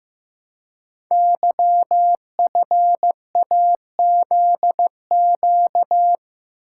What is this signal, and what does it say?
Morse code, heard as a keyed tone, about 15 wpm, 700 Hz. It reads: YFAZQ
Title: Code Groups - 5 Letter:Numbers - 15wpm